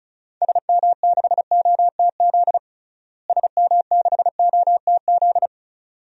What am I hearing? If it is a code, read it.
Morse code, heard as a keyed tone, about 36 wpm, 700 Hz. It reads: SM6OTZ SM6OTZ
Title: DXCC-100 Call Signs - 6 Characters - Repeated 2x - 35wpm